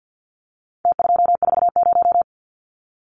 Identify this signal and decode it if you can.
Morse code, heard as a keyed tone, about 52 wpm, 700 Hz. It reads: T240
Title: Numbers - Contest Style - 4-Digit 50wpm